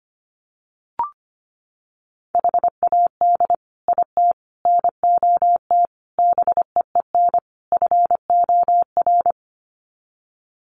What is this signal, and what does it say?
Morse code, heard as a keyed tone, about 25 wpm, 700 Hz. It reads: HAD IT NOT BEEN FOR
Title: Sets of 5 Words Encore2 25wpm